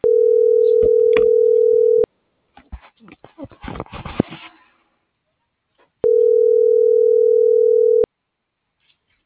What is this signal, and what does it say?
Ringback tone (North America), 440 + 480 Hz.